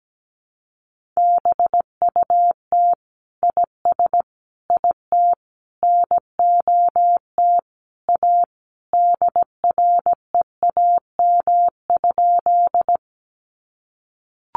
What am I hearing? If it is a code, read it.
Morse code, heard as a keyed tone, about 17 wpm, 700 Hz. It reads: BUT IS IT NOT A DREAM?
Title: Sentences from Top 1500 Words 17wpm